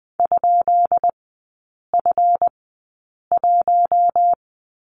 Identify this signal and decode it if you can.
Morse code, heard as a keyed tone, about 20 wpm, spaced as for 8 wpm, 700 Hz. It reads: ?F1
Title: 166 Lesson 27e - Number 7 - Three Characters 20wpm